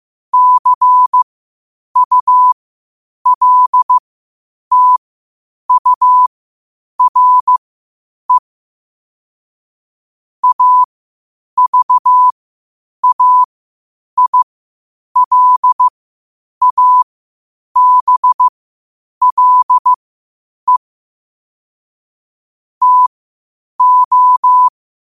Morse code, heard as a keyed tone, about 15 wpm, spaced as for 8 wpm, 1000 Hz. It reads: CULTURE AVAILABLE TO
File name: qotdfiles / 8wpm